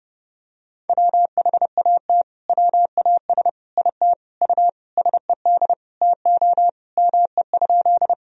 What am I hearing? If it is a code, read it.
Morse code, heard as a keyed tone, about 32 wpm, 700 Hz. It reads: WHAT WAS IT U SED TO ME?
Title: Sentences from Top 200 Words and Abbreviations 30wpm